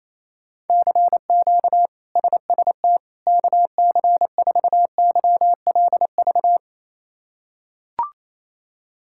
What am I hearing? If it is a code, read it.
Morse code, heard as a keyed tone, about 28 wpm, 700 Hz. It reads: CQ SST KC4YLV
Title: K1USN SST QSO Practice - 28wpm